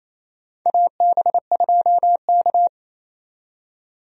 Morse code, heard as a keyed tone, about 27 wpm, 700 Hz. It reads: AB2K
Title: USA Call Signs - 4 Characters - 28wpm